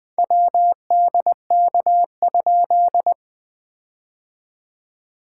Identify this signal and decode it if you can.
Morse code, heard as a keyed tone, about 20 wpm, 700 Hz. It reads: WDK?
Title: POTA QSO Practice - 20wpm